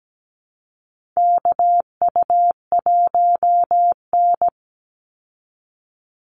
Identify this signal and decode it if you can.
Morse code, heard as a keyed tone, about 17 wpm, 700 Hz. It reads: KU1N
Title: Call Signs Repeated 2x - 17wpm